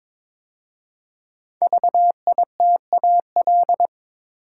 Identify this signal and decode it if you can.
Morse code, heard as a keyed tone, about 22 wpm, 700 Hz. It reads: VITAL
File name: 5 Letter Words - First 3 Letters Given - 22wpm